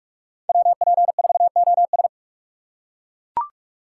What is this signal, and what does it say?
Morse code, heard as a keyed tone, about 41 wpm, 700 Hz. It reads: WP4OS
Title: USA Call Signs - 5 Characters - 2x Character Spacing - 45wpm